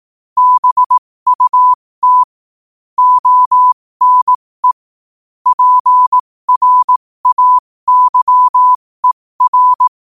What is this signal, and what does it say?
Morse code, heard as a keyed tone, about 18 wpm, spaced as for 15 wpm, 1000 Hz. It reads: BUT ONE PRAYER
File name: qotdfiles / 15wpm